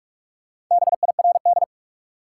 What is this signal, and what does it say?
Morse code, heard as a keyed tone, about 41 wpm, 700 Hz. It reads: BIRD
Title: Sentences from Top 400 Words - New Word Drill - 45wpm